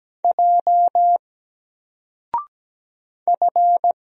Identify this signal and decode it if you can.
Morse code, heard as a keyed tone, about 17 wpm, 700 Hz. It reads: J F
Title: Single Character (Letter:Number:Punctuation:Prosign) 17wpm